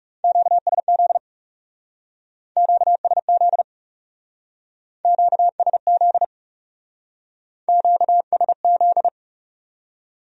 Morse code, heard as a keyed tone, about 37 wpm, 700 Hz. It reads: QSZ QSZ QSZ QSZ
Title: Speed-Racing Q-Codes - 30wpm